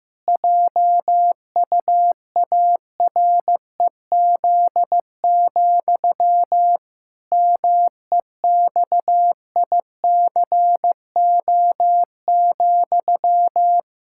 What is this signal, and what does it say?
Morse code, heard as a keyed tone, about 15 wpm, 700 Hz. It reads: JUAREZ, MEXICO,
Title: Pangram Challenge v2 - 15wpm